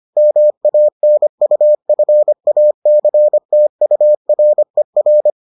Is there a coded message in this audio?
Yes: Morse code from 0.2 to 5.3 s, heard as a keyed tone, about 25 wpm, 600 Hz. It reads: MANUFACTURER